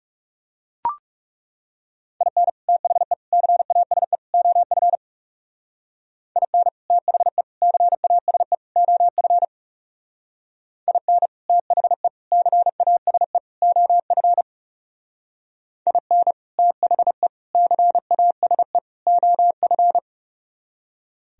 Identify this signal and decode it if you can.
Morse code, heard as a keyed tone, about 39 wpm, 700 Hz. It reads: IN THE CASE OF IN THE CASE OF IN THE CASE OF IN THE CASE OF